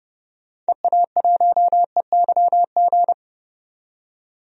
Morse code, heard as a keyed tone, about 32 wpm, 700 Hz. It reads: EA1EYG
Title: DXCC-100 Call Signs - 6 Characters - 30wpm